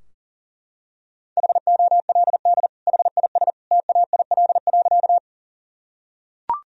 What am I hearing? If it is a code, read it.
Morse code, heard as a keyed tone, about 37 wpm, 700 Hz. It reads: HOLD HIS TAIL.
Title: Sentences from Top 1000 Words - Review Part 3 - 40wpm